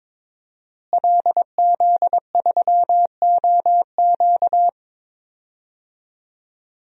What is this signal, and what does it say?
Morse code, heard as a keyed tone, about 22 wpm, 700 Hz. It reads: LZ3OQ
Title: DXCC-100 Call Signs - 5 Characters - Repeated 3x - 22wpm